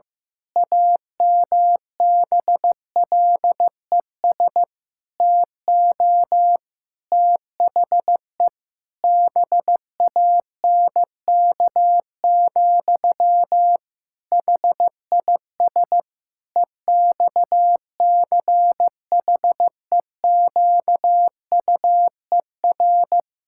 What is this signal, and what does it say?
Morse code, heard as a keyed tone, about 15 wpm, 700 Hz. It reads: AMBLES TO THE BANK, HIS EXCHEQUER
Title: Pangram Challenge v2 - 15wpm